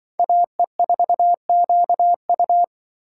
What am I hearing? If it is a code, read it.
Morse code, heard as a keyed tone, about 24 wpm, 700 Hz. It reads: AE4QU
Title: Speed Racing - USA Call Signs 20wpm